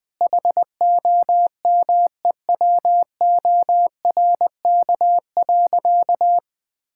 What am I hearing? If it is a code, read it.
Morse code, heard as a keyed tone, about 21 wpm, 700 Hz. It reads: HOMEWORK.
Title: Sentences with 3-Word Phrases - 20wpm